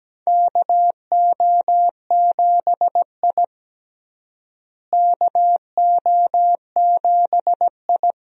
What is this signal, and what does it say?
Morse code, heard as a keyed tone, about 17 wpm, 700 Hz. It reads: KO7I KO7I
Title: USA Call Signs - 4 Characters - Repeated 2x - 17wpm